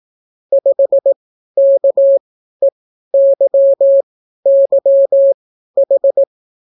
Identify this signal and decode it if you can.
Morse code, heard as a keyed tone, about 18 wpm, spaced as for 12 wpm, 550 Hz. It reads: 5KEYYH